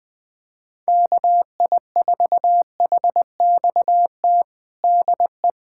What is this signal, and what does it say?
Morse code, heard as a keyed tone, about 20 wpm, 700 Hz. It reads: KI4HXT DE
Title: USA Callsign DE Callsign - 6 Characters - 20wpm